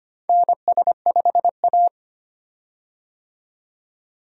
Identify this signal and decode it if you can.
Morse code, heard as a keyed tone, about 25 wpm, 700 Hz. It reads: NS5A
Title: USA Call Signs - 4 Characters - 25wpm